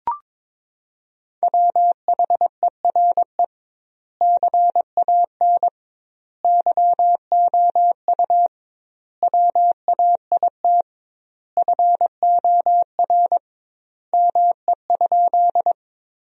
Morse code, heard as a keyed tone, about 22 wpm, 700 Hz. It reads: WHERE CAN YOU WAIT FOR ME?
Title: 2x Word-Spacing - Sentences from Top 500 Words - 22wpm